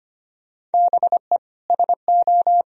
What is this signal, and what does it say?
Morse code, heard as a keyed tone, about 25 wpm, 700 Hz. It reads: BE SO
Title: Sentences from Top 1000 Words - Review Part 1 - 25wpm